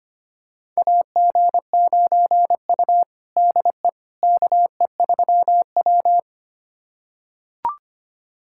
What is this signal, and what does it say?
Morse code, heard as a keyed tone, about 25 wpm, 700 Hz. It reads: AG9U DE KE3W
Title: USA Callsign DE Callsign - 4 Characters - 25wpm